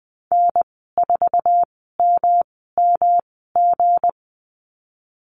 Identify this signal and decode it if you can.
Morse code, heard as a keyed tone, about 20 wpm, spaced as for 15 wpm, 700 Hz. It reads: N4MMG
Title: 219 Lesson 35g - Number 0 - Call Signs 20wpm